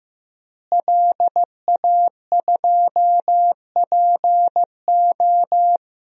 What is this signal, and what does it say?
Morse code, heard as a keyed tone, about 15 wpm, 700 Hz. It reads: LA2PO
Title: DXCC-100 Call Signs - 5 Characters - Repeated 3x - 15wpm